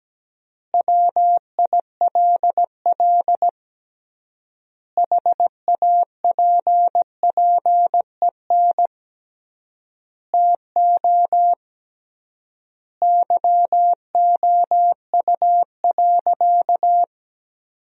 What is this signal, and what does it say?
Morse code, heard as a keyed tone, about 17 wpm, 700 Hz. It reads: WILL HAPPEN TO YOU.
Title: 3x Word-Spacing - Sentences from Top 500 Words - 17wpm